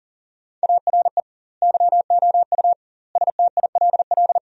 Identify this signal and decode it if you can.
Morse code, heard as a keyed tone, about 40 wpm, 700 Hz. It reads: ARE YOU STILL
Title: 2x Word-Spacing - Sentences from Top 400 Words - 40wpm